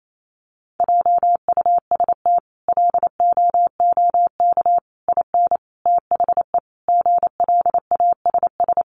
Morse code, heard as a keyed tone, about 28 wpm, 700 Hz. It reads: JUST LOOK IN THE GLASS
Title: Sentences from Top 1300 Words 28wpm